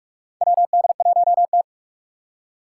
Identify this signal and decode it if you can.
Morse code, heard as a keyed tone, about 44 wpm, 700 Hz. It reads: WD1T
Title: Four Challenging Letter-Number Combinations 45wpm